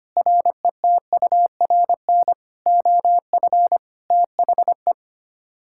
Morse code, heard as a keyed tone, about 25 wpm, 700 Hz. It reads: RETURN OF THE
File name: Sets of 9 Words - 25wpm